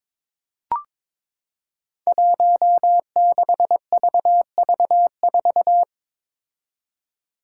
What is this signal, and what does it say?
Morse code, heard as a keyed tone, about 22 wpm, 700 Hz. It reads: 16VV4
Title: Five Challenging Letter-Number Combinations 22wpm